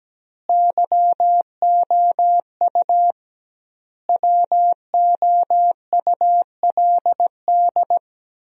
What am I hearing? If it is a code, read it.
Morse code, heard as a keyed tone, about 17 wpm, 700 Hz. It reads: YOU WOULD
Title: 2x Word-Spacing - Sentences from Top 300 Words - 17wpm